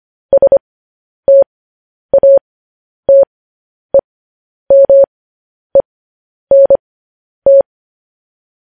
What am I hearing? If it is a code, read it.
Morse code, heard as a keyed tone, about 25 wpm, spaced as for 10 wpm, 550 Hz. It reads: STATEMENT
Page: news-headlines-in-morse-code-at-10-wpm